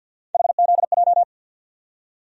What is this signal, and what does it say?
Morse code, heard as a keyed tone, about 47 wpm, 700 Hz. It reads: HZJ
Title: USA Call Sign Postfixes - 3 Letters 50wpm